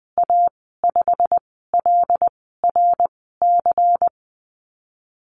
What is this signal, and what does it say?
Morse code, heard as a keyed tone, about 20 wpm, 700 Hz. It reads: A 5 L R C
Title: 179 Lesson 29g - Slash - Call Signs 20wpm